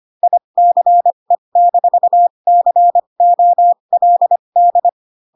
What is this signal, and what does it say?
Morse code, heard as a keyed tone, about 25 wpm, spaced as for 22 wpm, 700 Hz. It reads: ICE-COLD
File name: Pinocchio / Pinocchio ch6 18 25 ik1hge